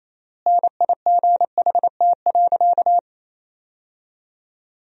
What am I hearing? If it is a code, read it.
Morse code, heard as a keyed tone, about 28 wpm, 700 Hz. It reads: NIGHT.
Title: 4x Word-Spacing - Sentences from Top 300 Words - 28wpm